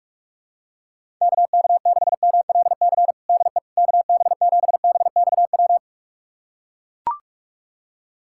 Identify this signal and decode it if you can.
Morse code, heard as a keyed tone, about 41 wpm, 700 Hz. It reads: KK6MLC DE KB7BXW
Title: USA Callsign DE Callsign 45wpm